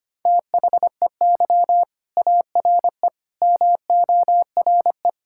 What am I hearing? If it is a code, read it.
Morse code, heard as a keyed tone, about 25 wpm, 700 Hz. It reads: THEY ARE MORE
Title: Sentences from Top 1000 Words - Review Part 5 - 25wpm